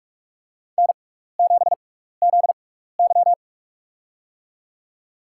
Morse code, heard as a keyed tone, about 46 wpm, spaced as for 16 wpm, 700 Hz. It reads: N7ZY
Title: USA Call Signs - 4 Characters - 6x Character Spacing - 45wpm